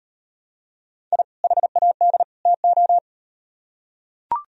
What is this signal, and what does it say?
Morse code, heard as a keyed tone, about 38 wpm, 700 Hz. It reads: I HAD TO